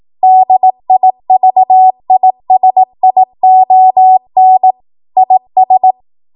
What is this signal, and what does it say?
Morse code, heard as a keyed tone, about 18 wpm, 750 Hz. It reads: DIVISION IS
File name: Archive / 18 WPM / 18WPM